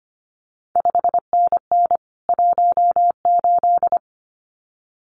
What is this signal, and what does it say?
Morse code, heard as a keyed tone, about 25 wpm, 700 Hz. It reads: 5NN 18
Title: CQ WW DX Contest Exchange Encore - 25wpm